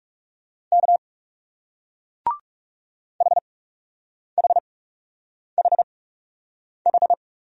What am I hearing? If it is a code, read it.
Morse code, heard as a keyed tone, about 38 wpm, spaced as for 8 wpm, 700 Hz. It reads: K HHHH